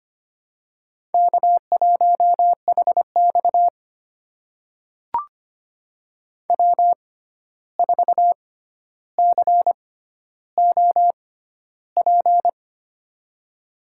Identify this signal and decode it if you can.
Morse code, heard as a keyed tone, about 25 wpm, spaced as for 12 wpm, 700 Hz. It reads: K1HX W4COP
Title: USA Call Signs - 6x Character Spacing - 25wpm